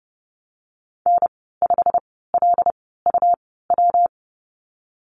Morse code, heard as a keyed tone, about 30 wpm, spaced as for 17 wpm, 700 Hz. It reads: N5LUW
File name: Lesson 19g - Letter w - Call Signs 30wpm